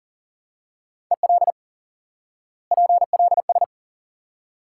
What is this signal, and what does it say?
Morse code, heard as a keyed tone, about 40 wpm, 700 Hz. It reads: EL PLS
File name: IQE Course 3 - Lesson 89 - YAGI - Two Elements - 40wpm